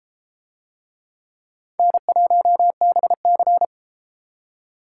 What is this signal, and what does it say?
Morse code, heard as a keyed tone, about 31 wpm, 700 Hz. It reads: N1BC